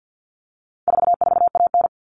Morse code, heard as a keyed tone, about 50 wpm, 700 Hz. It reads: 44AN
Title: All Cut Numbers - 4-Digit 50wpm